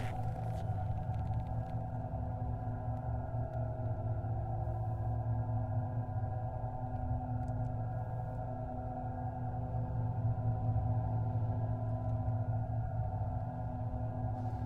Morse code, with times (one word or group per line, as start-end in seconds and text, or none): none